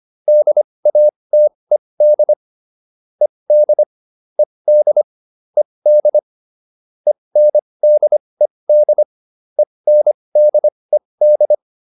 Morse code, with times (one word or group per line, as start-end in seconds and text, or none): 0.3-2.3 DATED
3.2-3.8 ED
4.4-5.0 ED
5.6-6.2 ED
7.1-9.0 ENDED
9.6-11.5 ENDED